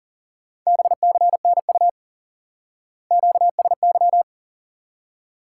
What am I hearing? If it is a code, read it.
Morse code, heard as a keyed tone, about 37 wpm, 700 Hz. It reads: BCNU QSY
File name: IQE Course 3 - Lesson 89 - YAGI - Two Elements - 40wpm